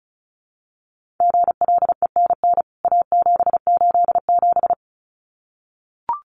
Morse code, heard as a keyed tone, about 34 wpm, 700 Hz. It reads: GLENN A787
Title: CWT Exchange 35wpm